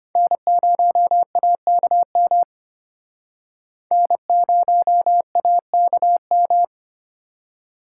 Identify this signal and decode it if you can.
Morse code, heard as a keyed tone, about 31 wpm, 700 Hz. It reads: N0AKM N0AKM
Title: Speed Racing - USA 1x3 Call Signs 25wpm